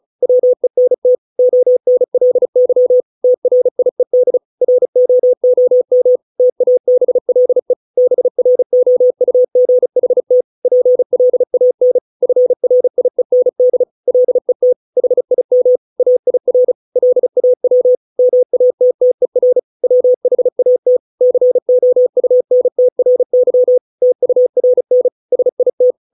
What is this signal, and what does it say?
Morse code, heard as a keyed tone, about 34 wpm, 500 Hz. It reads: WENT ONLY TRIED ROOM TABLE BROUGHT PLAN FRIEND LET HIM AIR LAW MATTER WHAT COUNTRY TURN SIT